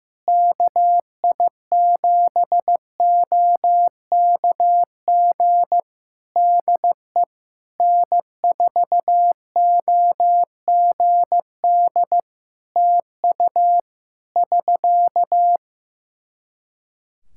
Morse code, heard as a keyed tone, about 15 wpm, 700 Hz. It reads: KI7OKG DE N4OGD TU <SK>
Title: K1USN SST QSO Practice 15wpm